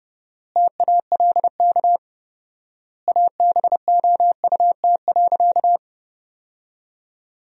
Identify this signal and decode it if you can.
Morse code, heard as a keyed tone, about 30 wpm, 700 Hz. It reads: TALK ABOUT.
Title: 4x Word-Spacing - Sentences from Top 400 Words - 30wpm